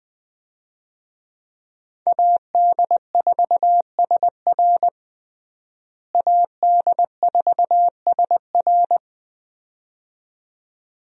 Morse code, heard as a keyed tone, about 21 wpm, 700 Hz. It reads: AD4SR AD4SR
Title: USA Call Signs - 5 Characters - Repeated 2x - 20wpm